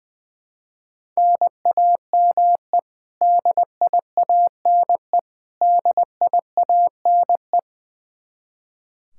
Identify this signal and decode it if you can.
Morse code, heard as a keyed tone, about 21 wpm, 700 Hz. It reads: NAME DIANE DIANE
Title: Practice QSO - Exchange 1 - Beginning - 20wpm